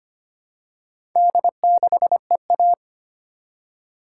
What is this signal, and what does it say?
Morse code, heard as a keyed tone, about 25 wpm, 700 Hz. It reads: D6EA
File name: practice / 25wpm / Four Letter-Number Combinations - 25wpm